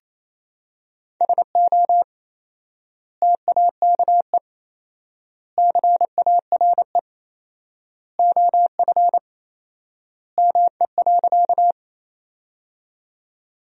Morse code, heard as a keyed tone, about 28 wpm, 700 Hz. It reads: SO TAKE CARE OF ME.
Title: 4x Word-Spacing - Sentences from Top 300 Words - 28wpm